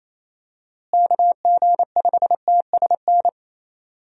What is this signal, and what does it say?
Morse code, heard as a keyed tone, about 28 wpm, 700 Hz. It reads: KG5TSN
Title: USA Call Signs - 6 Characters - Repeated 2x - 28wpm